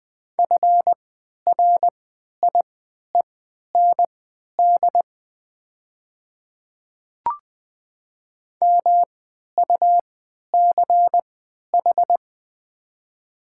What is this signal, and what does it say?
Morse code, heard as a keyed tone, about 20 wpm, spaced as for 11 wpm, 690 Hz. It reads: FRIEND MUCH
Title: 3x Character Spacing - Top 500 Words - 20wpm